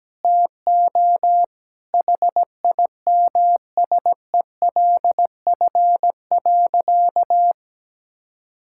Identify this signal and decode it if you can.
Morse code, heard as a keyed tone, about 17 wpm, 700 Hz. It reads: TO HIMSELF.
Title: Sentences from Top 1600 Words 17wpm